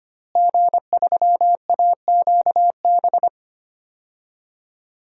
Morse code, heard as a keyed tone, about 25 wpm, 700 Hz. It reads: G3AQB
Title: DXCC-100 Call Signs - 5 Characters - Repeated 2x - 25wpm